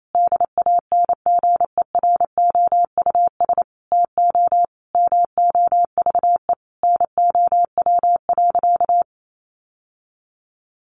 Morse code, heard as a keyed tone, about 28 wpm, 700 Hz. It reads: DANGEROUS TO MOVE NOW.
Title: Sentences from Top 1200 Words 28wpm